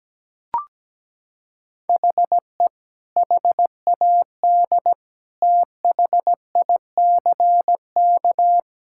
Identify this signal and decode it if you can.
Morse code, heard as a keyed tone, about 17 wpm, 700 Hz. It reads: HE HAD THICK
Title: Sentences from Top 1600 Words 17wpm